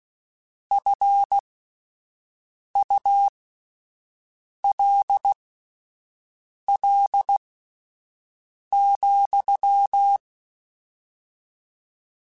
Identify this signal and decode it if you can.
Morse code, heard as a keyed tone, about 16 wpm, spaced as for 5 wpm, 800 Hz. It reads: FULL,
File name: Spaced / Chuckles Spaced-mp3s / HalfFull sp @10